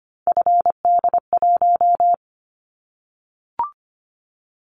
Morse code, heard as a keyed tone, about 25 wpm, 700 Hz. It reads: FD1
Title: Callsign Prefixes for Top 25 DXCC - 25wpm